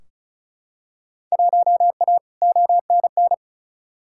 Morse code, heard as a keyed tone, about 36 wpm, 700 Hz. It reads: 1A ONN
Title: Field Day Exchanges Encore - 35wpm